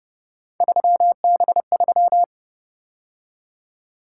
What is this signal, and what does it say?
Morse code, heard as a keyed tone, about 31 wpm, 700 Hz. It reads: 3B3
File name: Speed-Racing - Three Letter-Number Combinations - 25wpm